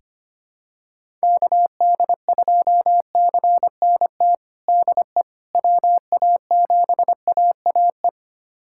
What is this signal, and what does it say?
Morse code, heard as a keyed tone, about 25 wpm, 700 Hz. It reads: KD2CNT DE WA7AAE
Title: USA Callsign DE Callsign - 6 Characters - 25wpm